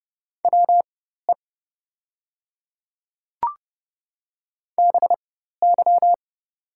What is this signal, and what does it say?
Morse code, heard as a keyed tone, about 30 wpm, spaced as for 14 wpm, 700 Hz. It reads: WE BY
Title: 4x Character Spacing - Top 100 Words - 30wpm